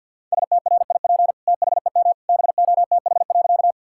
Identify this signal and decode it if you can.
Morse code, heard as a keyed tone, about 52 wpm, 700 Hz. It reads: STRIP THEM BOTH.
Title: Sentences from Top 1600 Words 50wpm